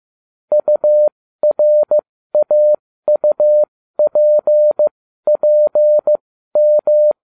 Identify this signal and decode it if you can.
Morse code, heard as a keyed tone, about 15 wpm, spaced as for 13 wpm, 600 Hz. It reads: URAUPPM